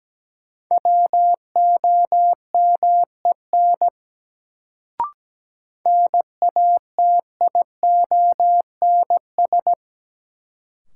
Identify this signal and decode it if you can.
Morse code, heard as a keyed tone, about 17 wpm, 700 Hz. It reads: WOMEN NATIONS
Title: Sets of 3 Words - New Word Drill - 17wpm